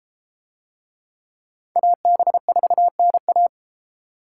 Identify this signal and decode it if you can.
Morse code, heard as a keyed tone, about 33 wpm, 700 Hz. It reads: AB4NA